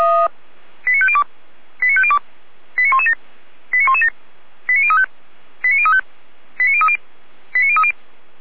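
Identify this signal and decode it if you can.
Five-tone selective call (ZVEI), heard as quick runs of five tones, tones of about 70 ms. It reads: 89502 89502 89117 89117 89035 89035 89030 89030
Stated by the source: ZVEI five-tone call 89115 received by SDR.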